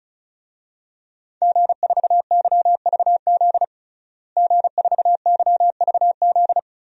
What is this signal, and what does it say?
Morse code, heard as a keyed tone, about 34 wpm, 700 Hz. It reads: G4YVZ G4YVZ
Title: DXCC-100 Call Signs - 5 Characters - Repeated 2x - 35wpm